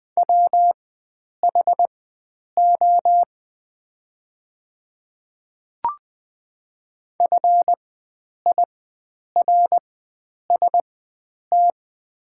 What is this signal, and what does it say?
Morse code, heard as a keyed tone, about 20 wpm, spaced as for 9 wpm, 700 Hz. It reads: WHO FIRST
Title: 4x Character Spacing - Top 100 Words - 20wpm